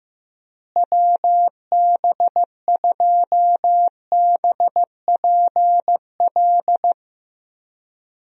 Morse code, heard as a keyed tone, about 16 wpm, 700 Hz. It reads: WB2BPL
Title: USA Call Signs - 6 Characters - Repeated 3x - 15wpm